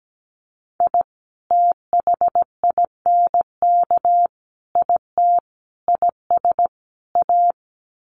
Morse code, heard as a keyed tone, about 17 wpm, 700 Hz. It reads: I THINK IT IS A
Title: Sets of 5 Words Encore 17wpm